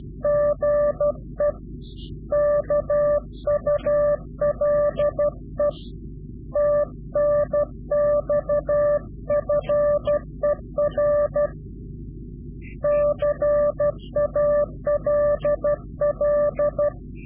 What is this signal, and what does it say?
Morse code, heard as a keyed tone, about 12 wpm, 590 Hz. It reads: GE KULE TNX FER CALL
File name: CW-QSO / Directional Call / Directional QSO